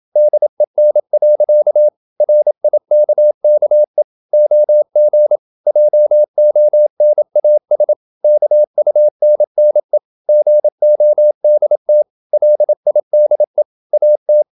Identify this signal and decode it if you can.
Morse code, heard as a keyed tone, about 27 wpm, 600 Hz. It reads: DEN. RIKKE OG JONAS KUNNE GODT LIDE AT
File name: Ferie QRQ 27wpm